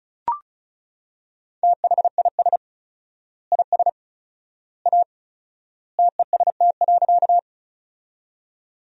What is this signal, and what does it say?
Morse code, heard as a keyed tone, about 36 wpm, 700 Hz. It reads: THIS IS A TEST.
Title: 4x Word-Spacing - Sentences from Top 500 Words - 35wpm